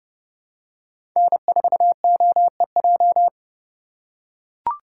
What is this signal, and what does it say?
Morse code, heard as a keyed tone, about 30 wpm, 700 Hz. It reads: N4OEJ
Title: USA Call Signs - 3x Character Spacing - 30wpm